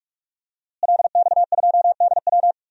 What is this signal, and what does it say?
Morse code, heard as a keyed tone, about 43 wpm, 700 Hz. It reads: LX2DW
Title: DXCC-100 Call Signs - 5 Characters - Repeated 2x - 45wpm